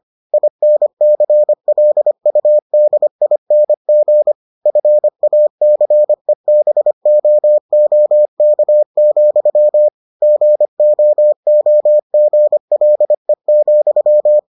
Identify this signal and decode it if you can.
Morse code, heard as a keyed tone, about 25 wpm, 600 Hz. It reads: INCLUDING FACEBOOK, GOOGLE,